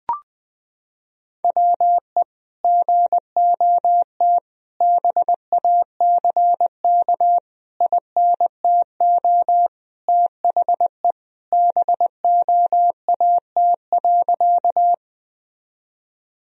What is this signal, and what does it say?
Morse code, heard as a keyed tone, about 20 wpm, 700 Hz. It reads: WE GOT BACK INTO THE BOAT.
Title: Sentences from Top 1000 Words - Review Part 3 - 20wpm